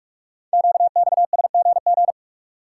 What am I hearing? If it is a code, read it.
Morse code, heard as a keyed tone, about 44 wpm, 700 Hz. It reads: QXSGG
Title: Code Groups - 5 Letter:Numbers - 45wpm